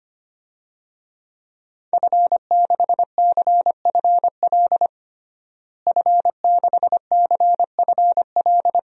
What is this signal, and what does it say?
Morse code, heard as a keyed tone, about 25 wpm, 700 Hz. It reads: F6CFL F6CFL
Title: DXCC-100 Call Signs - 5 Characters - Repeated 2x - 25wpm